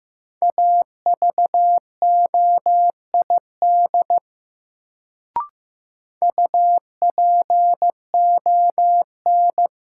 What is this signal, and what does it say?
Morse code, heard as a keyed tone, about 15 wpm, 700 Hz. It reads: AVOID UPON
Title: Sentences from Top 1000 Words - Review Part 6 - New Word Drill - 15wpm